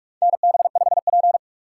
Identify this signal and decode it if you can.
Morse code, heard as a keyed tone, about 46 wpm, 690 Hz. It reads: NB5P
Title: USA Call Signs - 4 Characters - 3x Character Spacing - 45wpm